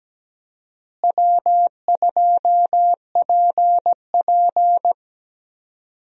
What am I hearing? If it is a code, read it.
Morse code, heard as a keyed tone, about 17 wpm, 700 Hz. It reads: W2PP
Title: USA Call Signs - 4 Characters - Repeated 3x - 17wpm